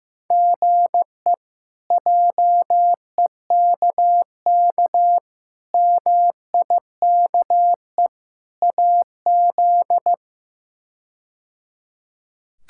Morse code, heard as a keyed tone, about 15 wpm, 700 Hz. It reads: GE JEKK MIKE AZ
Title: K1USN SST QSO Practice - 15wpm